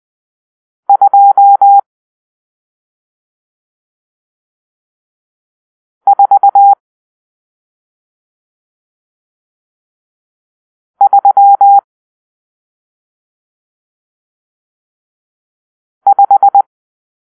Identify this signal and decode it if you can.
Morse code, heard as a keyed tone, about 20 wpm, 800 Hz. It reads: 2 4 3 5
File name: arrl / Grp-6 Drill-1